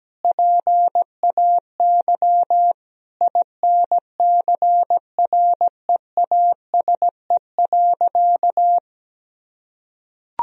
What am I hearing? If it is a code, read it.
Morse code, heard as a keyed tone, about 17 wpm, 700 Hz. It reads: PAY INCREASE.
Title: Sentences from Top 1000 Words - Review Part 4 - 17wpm